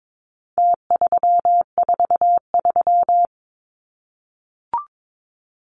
Morse code, heard as a keyed tone, about 22 wpm, 700 Hz. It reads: T343
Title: Numbers - Contest Style - 4-Digit 22wpm